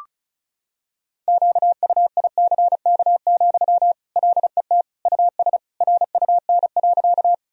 Morse code, heard as a keyed tone, about 36 wpm, 700 Hz. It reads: QUICK, LET US RUN.
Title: Sentences from Top 1000 Words - Review Part 1 - 35wpm